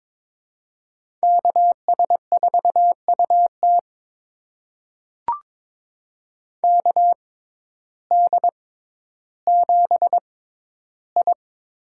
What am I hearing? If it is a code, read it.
Morse code, heard as a keyed tone, about 22 wpm, spaced as for 12 wpm, 700 Hz. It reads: KS4UT KD7I
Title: USA Call Signs - 5 Characters - 6x Character Spacing - 22wpm